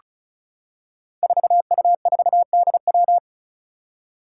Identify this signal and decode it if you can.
Morse code, heard as a keyed tone, about 36 wpm, 700 Hz. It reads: 4U4DW
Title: Five Challenging Letter-Number Combinations 35wpm